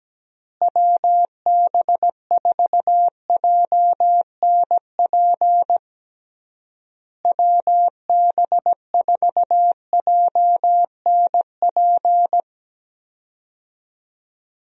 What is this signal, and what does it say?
Morse code, heard as a keyed tone, about 17 wpm, 700 Hz. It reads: WB4JNP WB4JNP
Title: USA Call Signs - 6 Characters - Repeated 3x - 17wpm